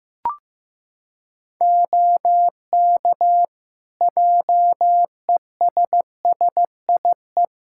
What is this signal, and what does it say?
Morse code, heard as a keyed tone, about 15 wpm, 700 Hz. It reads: OK JESSIE
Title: Practice QSO - Exchange 4 - Ending - 15wpm